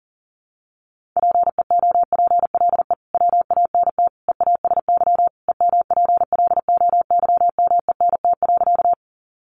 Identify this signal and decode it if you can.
Morse code, heard as a keyed tone, about 40 wpm, 700 Hz. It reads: PEOPLE WANT EASY EMPLOYMENT.
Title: Sentences from Top 1500 Words 40wpm